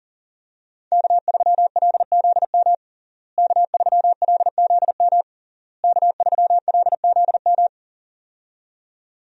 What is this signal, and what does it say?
Morse code, heard as a keyed tone, about 37 wpm, 700 Hz. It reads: K3LZM K3LZM K3LZM
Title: USA Call Signs - 5 Characters - Repeated 3x - 40wpm